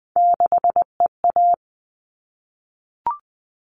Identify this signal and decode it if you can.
Morse code, heard as a keyed tone, about 20 wpm, 700 Hz. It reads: 6EA
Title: All Cut Numbers - 3-Digit 20wpm